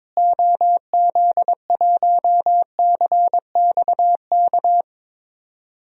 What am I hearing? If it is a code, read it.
Morse code, heard as a keyed tone, about 22 wpm, 700 Hz. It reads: OZ1CXK
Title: DXCC-100 Call Signs - 6 Characters - Repeated 3x - 22wpm